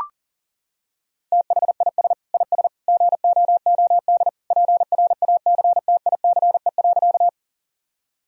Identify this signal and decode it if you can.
Morse code, heard as a keyed tone, about 37 wpm, 700 Hz. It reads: THIS IS GOOD PRACTICE.
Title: Sentences from Top 1000 Words - Review Part 4 - 40wpm